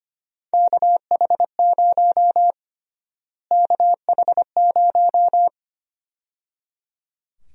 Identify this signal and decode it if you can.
Morse code, heard as a keyed tone, about 25 wpm, 700 Hz. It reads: KH0 KH0
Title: USA Call Sign Prefixes - Repeated 2x - 25wpm